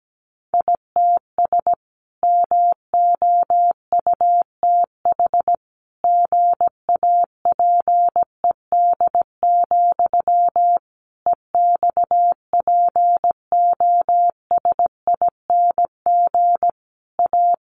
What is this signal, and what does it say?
Morse code, heard as a keyed tone, about 17 wpm, 700 Hz. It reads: ITS MOUTH GAPED, EXPOSING A
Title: A Sound Of Thunder 17wpm